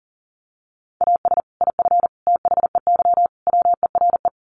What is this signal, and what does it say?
Morse code, heard as a keyed tone, about 37 wpm, 700 Hz. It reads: AS IF THEY WERE
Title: Sets of 4 Words 40wpm